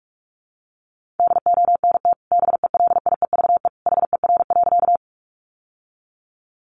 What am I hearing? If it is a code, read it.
Morse code, heard as a keyed tone, about 44 wpm, 700 Hz. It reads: DONT BELIEVE HER.
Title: Sentences from Top 1000 Words 45wpm